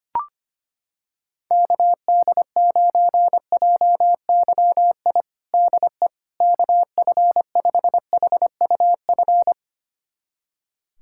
Morse code, heard as a keyed tone, about 25 wpm, 700 Hz. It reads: KD9JYI DE KF5HUF
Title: USA Callsign DE Callsign 25wpm